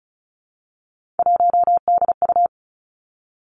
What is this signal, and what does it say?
Morse code, heard as a keyed tone, about 36 wpm, 700 Hz. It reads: 1DU
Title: Three Challenging Letter-Number Combinations 35wpm